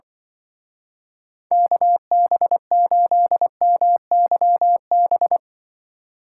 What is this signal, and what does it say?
Morse code, heard as a keyed tone, about 23 wpm, 700 Hz. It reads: KB8MYB
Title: Speed Racing - USA Call Signs 20wpm